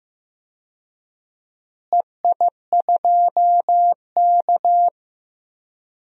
Morse code, heard as a keyed tone, about 15 wpm, 700 Hz. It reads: EI2K
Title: DXCC-100 Call Signs - 4 Characters - Repeated 3x - 15wpm